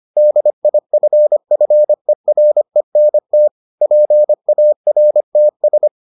Morse code, heard as a keyed tone, about 25 wpm, 600 Hz. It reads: DIFFERENT PARTS